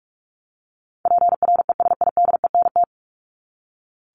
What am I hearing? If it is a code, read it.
Morse code, heard as a keyed tone, about 44 wpm, 700 Hz. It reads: PRESIDENT
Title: Top 751-1000 Words 45wpm